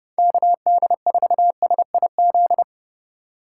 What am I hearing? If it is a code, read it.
Morse code, heard as a keyed tone, about 29 wpm, 700 Hz. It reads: KD4SIZ